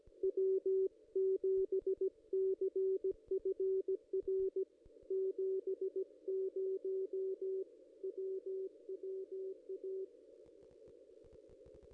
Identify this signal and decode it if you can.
Morse code, heard as a keyed tone, about 17 wpm, 380 Hz. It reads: W7CFR 70 WWA
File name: contests / ss-cw-k5tr-op-n5rz -1900